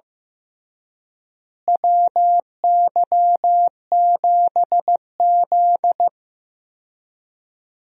Morse code, heard as a keyed tone, about 15 wpm, 700 Hz. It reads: WY7Z
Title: USA Call Signs - 4 Characters - Repeated 2x - 15wpm